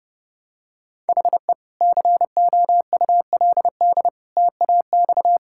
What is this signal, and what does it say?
Morse code, heard as a keyed tone, about 32 wpm, 700 Hz. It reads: HE COULD TAX
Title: Sentences from Top 1000 Words - Review Part 5 - 30wpm